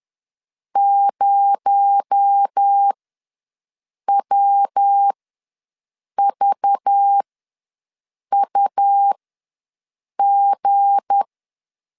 Morse code, heard as a keyed tone, about 10 wpm, spaced as for 6 wpm, 800 Hz. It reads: 0WVUG
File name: LEZIONI / 7 GF9Z / 7 GF ws 10wpm 3918